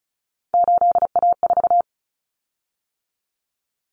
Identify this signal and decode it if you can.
Morse code, heard as a keyed tone, about 34 wpm, 700 Hz. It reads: 8A4
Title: Common Cut Numbers - 3-Digit 35wpm